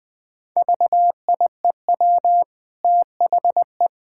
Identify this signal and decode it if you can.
Morse code, heard as a keyed tone, about 20 wpm, 700 Hz. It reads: VIEW THE
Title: Easy Sentences with Hints - 20wpm